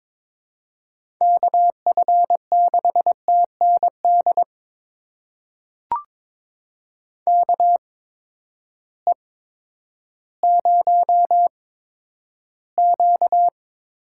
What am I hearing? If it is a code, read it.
Morse code, heard as a keyed tone, about 22 wpm, spaced as for 11 wpm, 700 Hz. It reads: KF6TND KE0Q
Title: USA Call Signs - 6 Characters - 8x Character Spacing - 22wpm